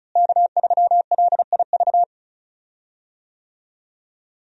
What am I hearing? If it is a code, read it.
Morse code, heard as a keyed tone, about 36 wpm, 700 Hz. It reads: K3LIV
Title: USA Call Signs - 5 Characters - Repeated 3x - 35wpm